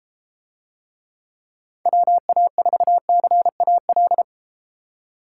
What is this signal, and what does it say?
Morse code, heard as a keyed tone, about 30 wpm, 700 Hz. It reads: WA4CAL